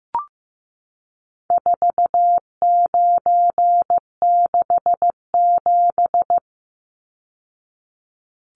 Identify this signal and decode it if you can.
Morse code, heard as a keyed tone, about 15 wpm, 700 Hz. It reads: 4967
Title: Numbers - 4-Digit 15wpm